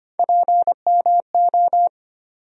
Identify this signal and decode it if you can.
Morse code, heard as a keyed tone, about 25 wpm, 700 Hz. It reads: PMO
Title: USA Call Sign Postfixes - 3 Letters 25wpm